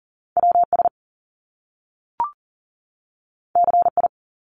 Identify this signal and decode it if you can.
Morse code, heard as a keyed tone, about 40 wpm, 700 Hz. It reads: WS CI
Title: Top Two Letter Combinations in Words 40wpm